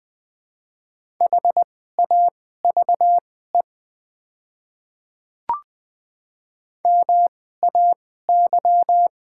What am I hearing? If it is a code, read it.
Morse code, heard as a keyed tone, about 20 wpm, spaced as for 14 wpm, 690 Hz. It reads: HAVE MAY
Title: 2x Character Spacing - Top 100 Words - 20wpm